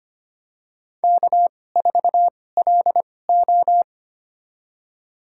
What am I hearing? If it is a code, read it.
Morse code, heard as a keyed tone, about 25 wpm, spaced as for 18 wpm, 700 Hz. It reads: K4LO
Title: USA Call Signs - 4 Characters - 2x Character Spacing - 25wpm